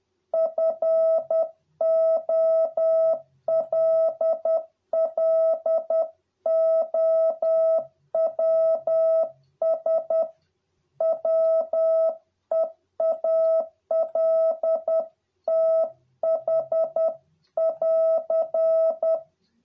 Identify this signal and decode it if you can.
Morse code, heard as a keyed tone, about 10 wpm, 650 Hz. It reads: FOLLOWS WEALTH <AR>